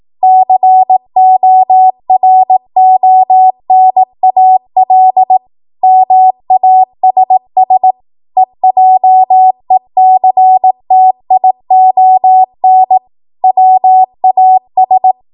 Morse code, heard as a keyed tone, about 18 wpm, 750 Hz. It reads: CORONAL MASS EJECTION WAS
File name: Morse / 18WPM